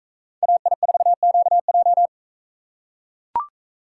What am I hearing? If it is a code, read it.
Morse code, heard as a keyed tone, about 44 wpm, 700 Hz. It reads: AI4QJ